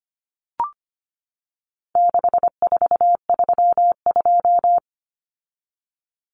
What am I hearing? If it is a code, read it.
Morse code, heard as a keyed tone, about 25 wpm, 700 Hz. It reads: 6432
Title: Common Cut Numbers - 4-Digit 25wpm